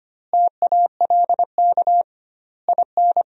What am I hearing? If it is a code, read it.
Morse code, heard as a keyed tone, about 25 wpm, 700 Hz. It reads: TALK IN
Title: 2x Word-Spacing - Sentences from Top 500 Words - 25wpm